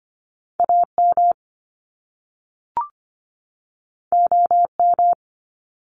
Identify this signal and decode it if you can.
Morse code, heard as a keyed tone, about 25 wpm, 700 Hz. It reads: AM OM
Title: Top Two Letter Combinations in Words 25wpm